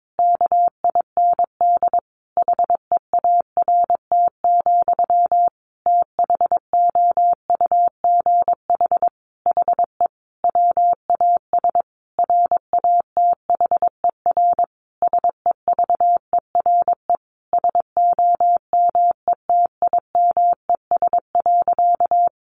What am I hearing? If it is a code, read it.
Morse code, heard as a keyed tone, about 22 wpm, 700 Hz. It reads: KIND HEART, THOUGH HE WAS RATHER SEVERE SOMETIMES.
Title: Sentences from Top 1400 Words 22wpm